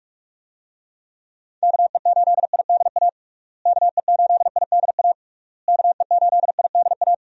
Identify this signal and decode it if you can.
Morse code, heard as a keyed tone, about 41 wpm, 700 Hz. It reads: KE8IDA KE8IDA KE8IDA
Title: USA Call Signs - 6 Characters - Repeated 3x - 45wpm